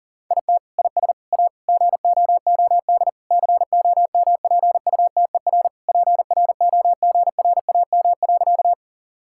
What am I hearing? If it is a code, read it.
Morse code, heard as a keyed tone, about 40 wpm, 700 Hz. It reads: IT IS A GOOD COMPUTER PROGRAM.
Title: Sentences from Top 1000 Words - Review Part 5 - 40wpm